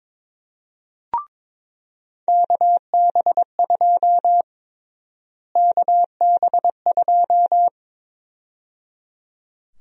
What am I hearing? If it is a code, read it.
Morse code, heard as a keyed tone, about 22 wpm, 700 Hz. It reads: KB2 KB2
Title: USA Call Sign Prefixes - Repeated 2x - 22wpm